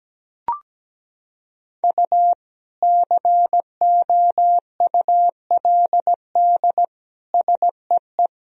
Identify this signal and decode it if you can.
Morse code, heard as a keyed tone, about 17 wpm, 700 Hz. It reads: U COULD SEE
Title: Sentences from Top 500 Words and Abbreviations 17wpm